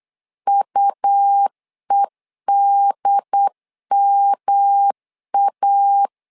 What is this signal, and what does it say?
Morse code, heard as a keyed tone, about 8 wpm, 800 Hz. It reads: UEDMA
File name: LEZIONI / 3 MWRO / 3 M 8wpm 2300